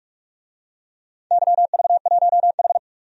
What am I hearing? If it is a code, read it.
Morse code, heard as a keyed tone, about 44 wpm, 700 Hz. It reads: YV1H
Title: DXCC-100 Call Signs - 4 Characters - 45wpm